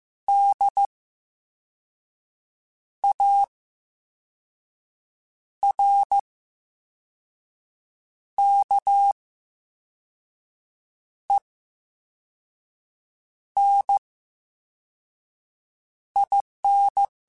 Morse code, heard as a keyed tone, about 15 wpm, 800 Hz. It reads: D A R K E N I N
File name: Special INGs-mp3s / Special INGs-mp3s / VerySlowToFaster Ing @13